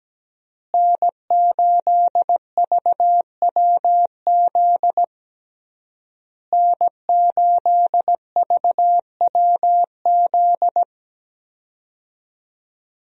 Morse code, heard as a keyed tone, about 17 wpm, 700 Hz. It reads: N8VWZ N8VWZ
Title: USA Call Signs - 5 Characters - Repeated 3x - 17wpm